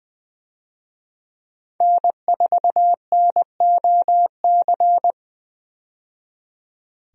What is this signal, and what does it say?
Morse code, heard as a keyed tone, about 20 wpm, 700 Hz. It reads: N4NOC
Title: USA Call Signs - 5 Characters - 20wpm